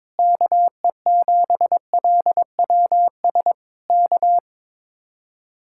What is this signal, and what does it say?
Morse code, heard as a keyed tone, about 21 wpm, 700 Hz. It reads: KE7LWS K
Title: Speed-Racing Common QSO Phrases Encore - 22wpm